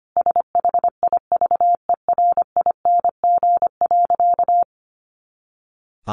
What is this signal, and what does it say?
Morse code, heard as a keyed tone, about 25 wpm, 700 Hz. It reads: SHIVERING.
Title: A Sound Of Thunder 25wpm